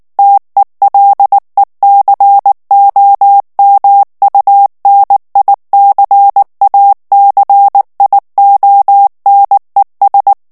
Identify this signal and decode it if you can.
Morse code, heard as a keyed tone, about 19 wpm, 800 Hz. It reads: TELECOMUNICACIONES
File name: morse mp3 / ESPAGNOL TEXTE EN CLAIR 2 VITESSE 18 MOTS MINUTES